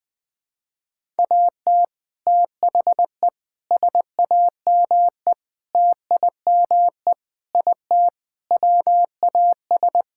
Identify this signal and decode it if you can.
Morse code, heard as a keyed tone, about 21 wpm, 700 Hz. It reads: AT THE SAME TIME IT WAS
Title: Sentences with 3-Word Phrases - 20wpm